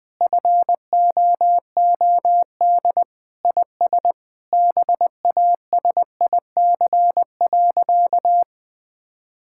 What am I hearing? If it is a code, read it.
Morse code, heard as a keyed tone, about 20 wpm, 700 Hz. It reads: FOOD IS BASIC.
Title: Sentences with 3-Word Phrases - 20wpm